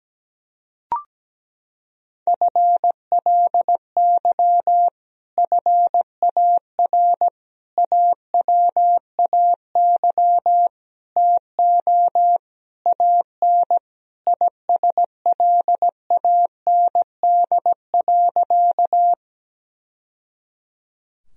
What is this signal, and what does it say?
Morse code, heard as a keyed tone, about 17 wpm, 700 Hz. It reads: FLY FAR AWAY TO AN ISLAND.
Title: Sentences from Top 1000 Words - Review Part 2 - 17wpm